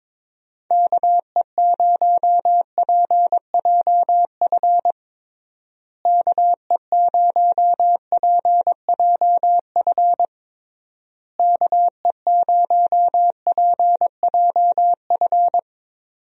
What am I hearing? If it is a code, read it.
Morse code, heard as a keyed tone, about 22 wpm, 700 Hz. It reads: KE0PJF KE0PJF KE0PJF
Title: USA Call Signs - 6 Characters - Repeated 3x - 22wpm